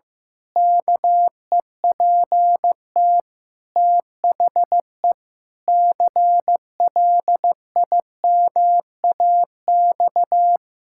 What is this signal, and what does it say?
Morse code, heard as a keyed tone, about 15 wpm, 700 Hz. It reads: KEPT THE CLIMAX
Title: Pangram Challenge v2 - 15wpm